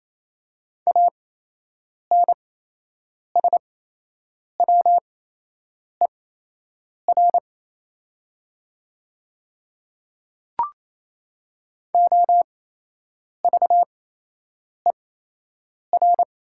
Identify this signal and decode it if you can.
Morse code, heard as a keyed tone, about 28 wpm, spaced as for 8 wpm, 700 Hz. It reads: ANSWER OVER
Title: 8x Character Spacing - Top 300 Words - 28wpm